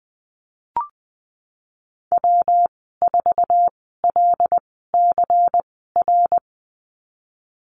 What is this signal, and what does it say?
Morse code, heard as a keyed tone, about 19 wpm, spaced as for 14 wpm, 700 Hz. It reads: W4LCR
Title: 140 Lesson 23g - Letter f - Call Signs 20wpm